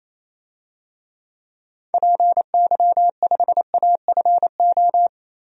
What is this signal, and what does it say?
Morse code, heard as a keyed tone, about 28 wpm, 700 Hz. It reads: PY5AFO
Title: DXCC-100 Call Signs - 6 Characters - 28wpm